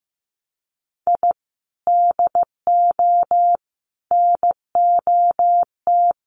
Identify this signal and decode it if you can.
Morse code, heard as a keyed tone, about 15 wpm, 700 Hz. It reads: I DO NOT
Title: Sets of 5 Words Encore 15wpm